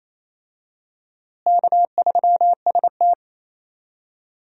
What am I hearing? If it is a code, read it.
Morse code, heard as a keyed tone, about 28 wpm, 700 Hz. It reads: K3ST
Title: USA Call Signs - 4 Characters - Repeated 2x - 28wpm